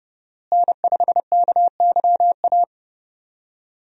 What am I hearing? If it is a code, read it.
Morse code, heard as a keyed tone, about 30 wpm, 700 Hz. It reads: N5KYA
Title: USA Call Signs - 5 Characters - Repeated 2x - 30wpm